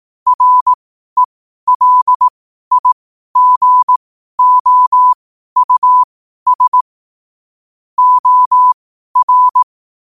Morse code, heard as a keyed tone, about 18 wpm, spaced as for 12 wpm, 1000 Hz. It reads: RELIGOUS OR
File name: qotdfiles / 12wpm